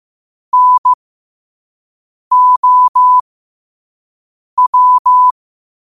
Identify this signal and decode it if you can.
Morse code, heard as a keyed tone, about 15 wpm, spaced as for 5 wpm, 1000 Hz. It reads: NOW
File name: qotdfiles / 5wpm